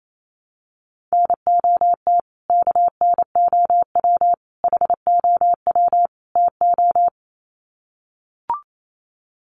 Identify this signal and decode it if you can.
Morse code, heard as a keyed tone, about 28 wpm, 700 Hz. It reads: NOT KNOW HOW TO
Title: Sets of 4 Words 28wpm